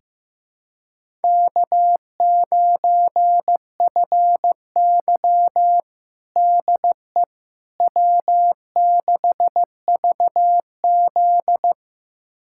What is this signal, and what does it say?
Morse code, heard as a keyed tone, about 15 wpm, 700 Hz. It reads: K9FY DE W6VZ
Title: USA Callsign DE Callsign - 4 Characters - 15wpm